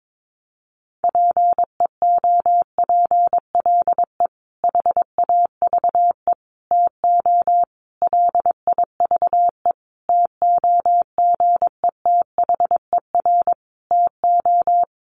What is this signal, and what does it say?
Morse code, heard as a keyed tone, about 22 wpm, 700 Hz. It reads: PEOPLE HAVE TO LIVE TOGETHER TO
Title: Sentences from Top 1100 Words 22wpm